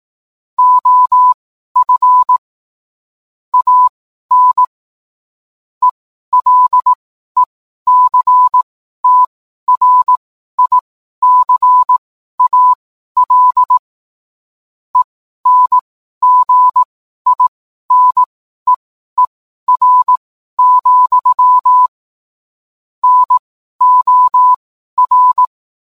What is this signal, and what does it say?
Morse code, heard as a keyed tone, about 18 wpm, spaced as for 12 wpm, 1000 Hz. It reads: OF AN ELECTRICAL ENGINEER, NOR